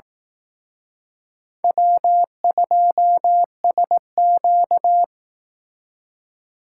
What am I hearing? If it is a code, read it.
Morse code, heard as a keyed tone, about 18 wpm, 700 Hz. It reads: W2SQ
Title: Speed Racing - USA 1x2 Call Signs 15wpm